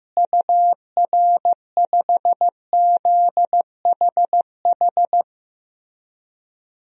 Morse code, heard as a keyed tone, about 16 wpm, 700 Hz. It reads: UR5ZHH
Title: DXCC-100 Call Signs - 6 Characters - Repeated 3x - 15wpm